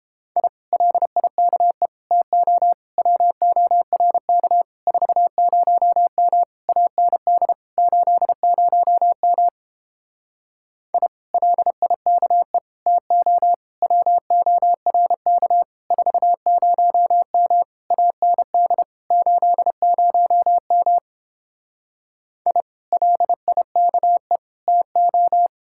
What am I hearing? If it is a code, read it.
Morse code, heard as a keyed tone, about 31 wpm, 700 Hz. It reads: I LIKE TO WORK 40M AND 80M I LIKE TO WORK 40M AND 80M I LIKE TO